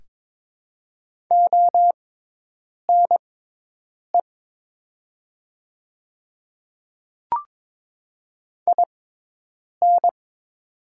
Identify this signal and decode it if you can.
Morse code, heard as a keyed tone, about 22 wpm, spaced as for 8 wpm, 700 Hz. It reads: ONE IN
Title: 6x Character Spacing - Top 100 Words - 22wpm